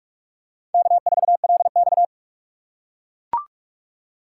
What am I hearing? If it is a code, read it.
Morse code, heard as a keyed tone, about 44 wpm, 700 Hz. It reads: K4LX